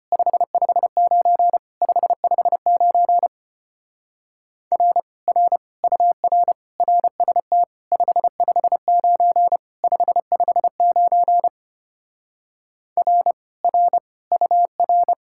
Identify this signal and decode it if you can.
Morse code, heard as a keyed tone, about 33 wpm, 700 Hz. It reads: 559 559 R R UR RST 559 559 R R UR